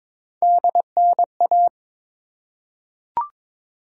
Morse code, heard as a keyed tone, about 22 wpm, 700 Hz. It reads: DNA
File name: practice / 22wpm / Three Letter-Number Combinations - 22wpm